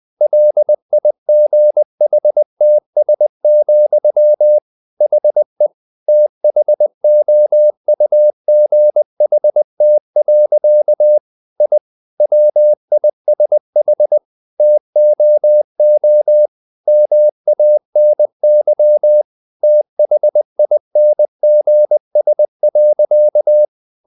Morse code, heard as a keyed tone, about 20 wpm, 600 Hz. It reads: LIGHTS, HE THOUGHT. I WISH TOO MANY THINGS.
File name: the old man and the sea part 2 20wpm